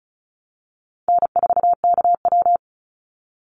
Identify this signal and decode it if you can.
Morse code, heard as a keyed tone, about 34 wpm, 700 Hz. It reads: N4KW
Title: Easy Call Signs Repeated 3x - 35wpm